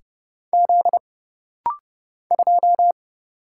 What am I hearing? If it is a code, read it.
Morse code, heard as a keyed tone, about 30 wpm, 700 Hz. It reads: Z 2
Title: Single Letter-Number Alt - Rapid-Fire 30wpm